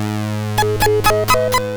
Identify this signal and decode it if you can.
There is no code.